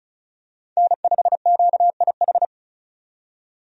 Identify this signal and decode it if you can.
Morse code, heard as a keyed tone, about 34 wpm, 700 Hz. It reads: N5QIH
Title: USA Call Signs - 5 Characters - 35wpm